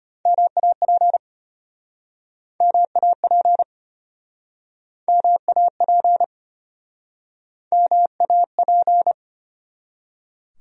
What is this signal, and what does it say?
Morse code, heard as a keyed tone, about 36 wpm, 700 Hz. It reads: MAP MAP MAP MAP